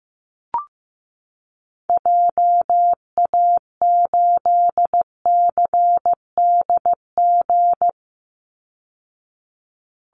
Morse code, heard as a keyed tone, about 15 wpm, 700 Hz. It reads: JA8CDG
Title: Call Signs Repeated 2x - 15wpm